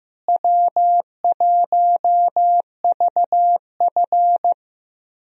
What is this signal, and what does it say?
Morse code, heard as a keyed tone, about 15 wpm, 700 Hz. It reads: W1VF
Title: USA Call Signs - 4 Characters - Repeated 3x - 15wpm